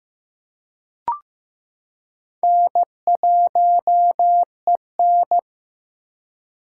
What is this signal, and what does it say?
Morse code, heard as a keyed tone, about 15 wpm, 700 Hz. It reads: N1EN
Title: K1USN SST QSO Practice - 15wpm